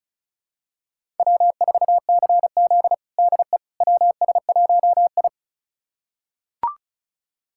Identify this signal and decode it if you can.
Morse code, heard as a keyed tone, about 34 wpm, 700 Hz. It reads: W4CZ DE WS1I
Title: USA Callsign DE Callsign - 4 Characters - 35wpm